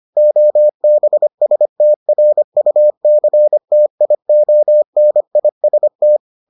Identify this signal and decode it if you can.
Morse code, heard as a keyed tone, about 25 wpm, 600 Hz. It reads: OBSTRUCTIONIST